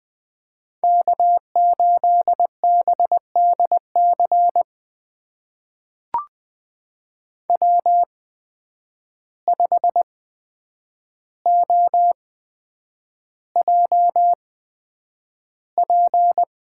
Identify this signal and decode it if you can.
Morse code, heard as a keyed tone, about 20 wpm, spaced as for 9 wpm, 700 Hz. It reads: K8BDC W5OJP
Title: USA Call Signs - 8x Character Spacing - 20wpm